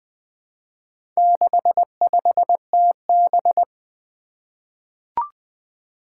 Morse code, heard as a keyed tone, about 20 wpm, 700 Hz. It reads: 65TB
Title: Four Challenging Letter-Number Combinations 20wpm